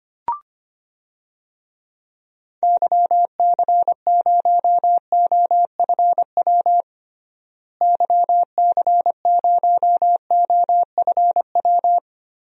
Morse code, heard as a keyed tone, about 25 wpm, 700 Hz. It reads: YC0OFW YC0OFW
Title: DXCC-100 Call Signs - 6 Characters - Repeated 3x - 25wpm